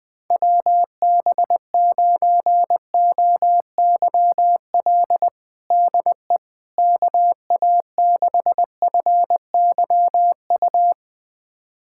Morse code, heard as a keyed tone, about 21 wpm, 700 Hz. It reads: WB9OYL DE KA6FYU
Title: USA Callsign DE Callsign - 6 Characters - 20wpm